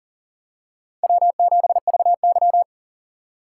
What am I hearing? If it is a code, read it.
Morse code, heard as a keyed tone, about 40 wpm, 700 Hz. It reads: W7VY
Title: USA Call Signs - 4 Characters - 4x Character Spacing - 40wpm